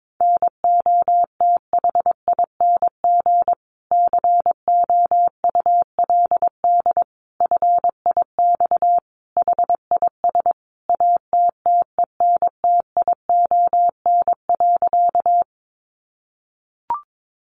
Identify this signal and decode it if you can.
Morse code, heard as a keyed tone, about 22 wpm, 700 Hz. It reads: NOTHING COULD FIX HIS ATTENTION.
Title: Sentences from Top 1400 Words 22wpm